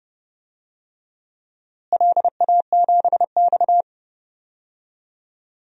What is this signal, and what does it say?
Morse code, heard as a keyed tone, about 31 wpm, 700 Hz. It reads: LA7X
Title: DXCC-100 Call Signs - 4 Characters - Repeated 2x - 30wpm